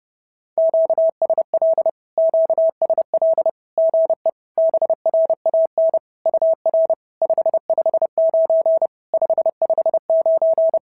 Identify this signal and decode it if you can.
Morse code, heard as a keyed tone, about 30 wpm, 660 Hz. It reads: QSL QSL GE BRAN UR 559 559
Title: SOTA QSO Practice 30wpm